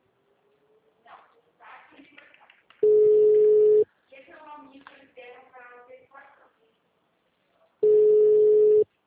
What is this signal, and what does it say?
Ringback tone (Europe, 425 Hz), with speech.